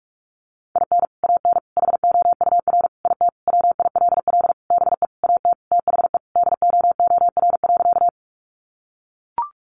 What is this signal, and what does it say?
Morse code, heard as a keyed tone, about 44 wpm, 700 Hz. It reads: IN AN HOUR IT WILL BE AT THE DOOR.
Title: Sentences from Top 400 Words 45wpm